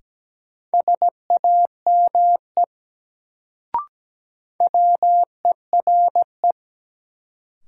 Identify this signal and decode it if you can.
Morse code, heard as a keyed tone, about 17 wpm, 700 Hz. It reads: SAME WERE
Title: Sets of 2 Words - New Word Drill - 17wpm